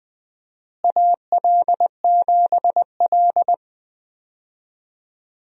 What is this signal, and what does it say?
Morse code, heard as a keyed tone, about 20 wpm, 700 Hz. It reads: AL7L
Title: USA Call Signs - 4 Characters - Repeated 3x - 20wpm